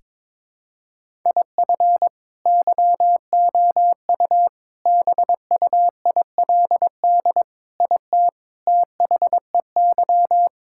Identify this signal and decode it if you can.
Morse code, heard as a keyed tone, about 22 wpm, 700 Hz. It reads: IF YOU BUILD IT THEY
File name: practice / 22wpm / Sets of 7 Words - 22wpm